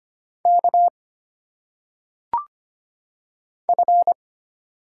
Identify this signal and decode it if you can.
Morse code, heard as a keyed tone, about 25 wpm, 700 Hz. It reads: K F
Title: Single Letters - Phonetic Pronunciation 25wpm